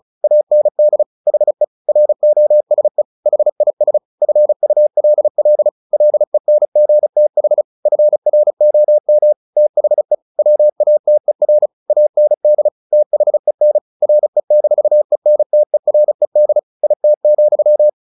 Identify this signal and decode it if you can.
Morse code, heard as a keyed tone, about 34 wpm, 600 Hz. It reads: AND HE ROSE HIS FULL LENGTH FROM THE WATER AND THEN RE-ENTERED IT,
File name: the old man and the sea / the old man and the sea 1 35wpm